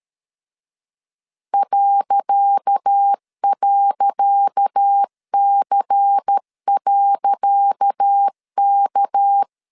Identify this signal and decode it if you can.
Morse code, heard as a keyed tone, about 13 wpm, 800 Hz. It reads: ..C.K
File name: LEZIONI / 10 SIGNS / 10C DOT 12wpm 2451